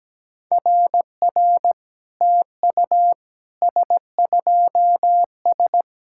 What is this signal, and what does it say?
Morse code, heard as a keyed tone, about 17 wpm, 700 Hz. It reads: RR TU S2S
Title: SOTA QSO Practice 17wpm